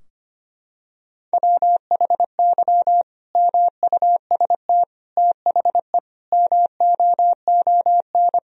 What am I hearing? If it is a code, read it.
Morse code, heard as a keyed tone, about 25 wpm, 700 Hz. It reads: WHY MUST THE MOON
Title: Sentences from Top 1000 Words - Review Part 3 - 25wpm